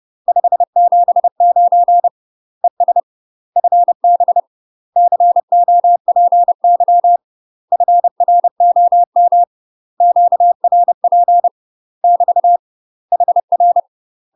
Morse code, heard as a keyed tone, about 30 wpm, 700 Hz. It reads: 579 ES FB COPY FROM QRP <BT> HR
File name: UK MASTER QSOs TRAINING / 30WPM / UK MASTER QSOs 0021